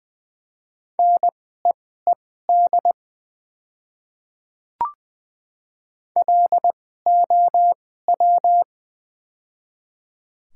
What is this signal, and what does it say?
Morse code, heard as a keyed tone, about 20 wpm, spaced as for 14 wpm, 700 Hz. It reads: NEED LOW
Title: 2x Character Spacing - Top 300 Words - 20wpm